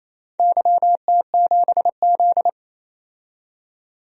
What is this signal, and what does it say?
Morse code, heard as a keyed tone, about 28 wpm, 700 Hz. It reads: YT7Z
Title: DXCC-100 Call Signs - 4 Characters - 28wpm